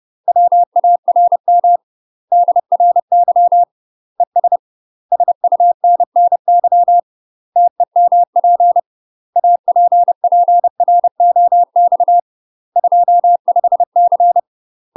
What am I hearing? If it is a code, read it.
Morse code, heard as a keyed tone, about 30 wpm, 700 Hz. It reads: WARM DRY ES SUNNY TEMP APPROX 25C
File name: UK MASTER QSOs TRAINING / 30WPM / UK MASTER QSOs 0013